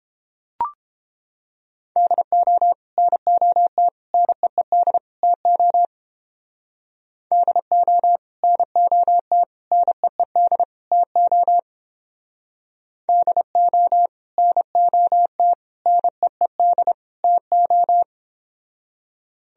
Morse code, heard as a keyed tone, about 30 wpm, 700 Hz. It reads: DO NOT NEED TO DO NOT NEED TO DO NOT NEED TO